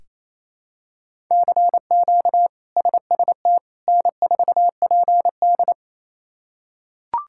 Morse code, heard as a keyed tone, about 27 wpm, 700 Hz. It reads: CQ SST N4PD
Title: K1USN SST QSO Practice - 28wpm